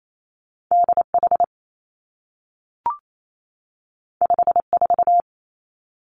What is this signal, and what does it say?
Morse code, heard as a keyed tone, about 28 wpm, 700 Hz. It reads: D H 5 4
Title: Two Challenging Letter-Number Combinations 28wpm